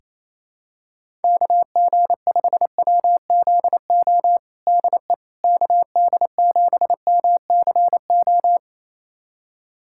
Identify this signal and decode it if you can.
Morse code, heard as a keyed tone, about 28 wpm, 700 Hz. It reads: KG5WZO DE KD7MCO
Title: USA Callsign DE Callsign 28wpm